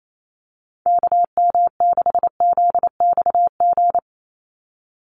Morse code, heard as a keyed tone, about 28 wpm, 700 Hz. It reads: KM6ZXG
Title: USA Call Signs 28wpm